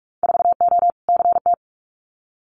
Morse code, heard as a keyed tone, about 46 wpm, 700 Hz. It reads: 4O CT
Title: Winter Field Day Exchanges Encore 45wpm